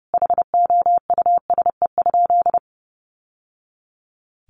Morse code, heard as a keyed tone, about 30 wpm, 700 Hz. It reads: HOUSE?
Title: Sentences from Top 900 Words 30wpm